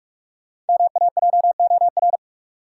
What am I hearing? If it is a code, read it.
Morse code, heard as a keyed tone, about 42 wpm, 700 Hz. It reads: MAJOR
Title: Sentences from Top 600 Words - New Word Drill - 45wpm